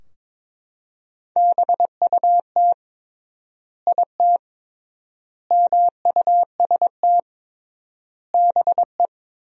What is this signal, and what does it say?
Morse code, heard as a keyed tone, about 22 wpm, 700 Hz. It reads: BUT IT MUST BE
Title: 3x Word-Spacing - Sentences from Top 500 Words - 22wpm